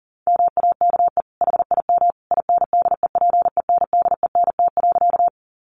Morse code, heard as a keyed tone, about 40 wpm, 700 Hz. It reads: MAKE HIM INDEPENDENT.
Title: Sentences from Top 900 Words 40wpm